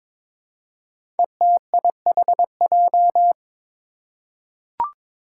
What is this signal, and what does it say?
Morse code, heard as a keyed tone, about 22 wpm, 700 Hz. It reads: ETIHJ
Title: Five Challenging Letter-Number Combinations 22wpm